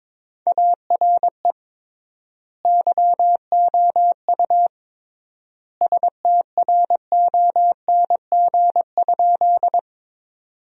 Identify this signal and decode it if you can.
Morse code, heard as a keyed tone, about 22 wpm, 700 Hz. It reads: ARE YOU STRONG?
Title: 3x Word-Spacing - Sentences from Top 500 Words - 22wpm